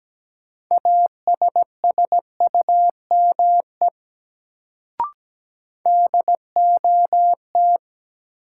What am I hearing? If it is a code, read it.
Morse code, heard as a keyed tone, about 17 wpm, 700 Hz. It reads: ASSUME DOT
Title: Sentences from Top 1500 Words - New Word Drill Alt - 17wpm